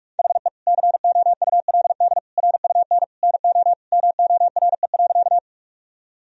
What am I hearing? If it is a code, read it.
Morse code, heard as a keyed tone, about 42 wpm, 700 Hz. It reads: HE COULD RUN NO MORE.
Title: Sentences from Top 1000 Words - Review Part 1 - 45wpm